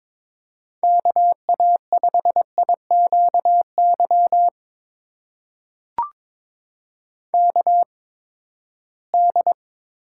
Morse code, heard as a keyed tone, about 22 wpm, spaced as for 15 wpm, 700 Hz. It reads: KA5IQY KD
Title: USA Call Signs - 6 Characters - 8x Character Spacing - 22wpm